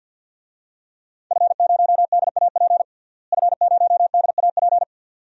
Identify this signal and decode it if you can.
Morse code, heard as a keyed tone, about 47 wpm, 700 Hz. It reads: F0DAP F0DAP
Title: DXCC-100 Call Signs - 5 Characters - Repeated 2x - 50wpm